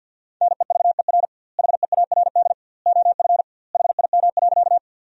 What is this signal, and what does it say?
Morse code, heard as a keyed tone, about 50 wpm, 700 Hz. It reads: NEVER HEARD OF HIM.
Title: 2x Word-Spacing - Sentences from Top 400 Words - 50wpm